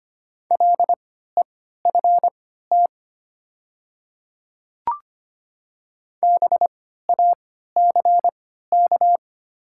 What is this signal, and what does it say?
Morse code, heard as a keyed tone, about 25 wpm, spaced as for 14 wpm, 690 Hz. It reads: LEFT BACK
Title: 3x Character Spacing - Top 500 Words - 25wpm